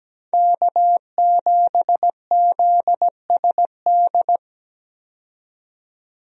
Morse code, heard as a keyed tone, about 17 wpm, 700 Hz. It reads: K7ZSD
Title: USA Call Signs - 5 Characters - 17wpm